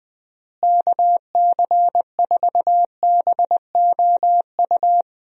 Morse code, heard as a keyed tone, about 20 wpm, 700 Hz. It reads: KC4BOU
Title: USA Call Signs - 6 Characters - Repeated 2x - 20wpm